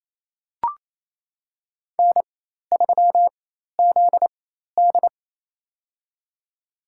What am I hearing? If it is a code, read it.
Morse code, heard as a keyed tone, about 27 wpm, spaced as for 13 wpm, 700 Hz. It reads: N3ZD
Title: USA Call Signs - 4 Characters - 4x Character Spacing - 28wpm